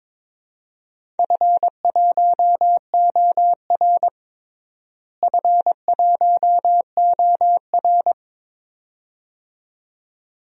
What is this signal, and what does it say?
Morse code, heard as a keyed tone, about 22 wpm, 700 Hz. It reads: F1OR F1OR
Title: DXCC-100 Call Signs - 4 Characters - Repeated 2x - 22wpm